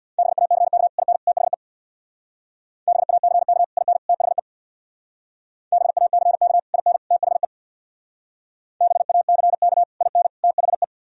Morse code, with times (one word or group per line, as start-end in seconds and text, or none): 0.2-0.9 BACK
1.0-1.2 IN
1.3-1.5 THE
2.9-3.6 BACK
3.8-4.0 IN
4.1-4.4 THE
5.7-6.6 BACK
6.7-7.0 IN
7.1-7.5 THE
8.8-9.8 BACK
10.0-10.3 IN
10.4-10.8 THE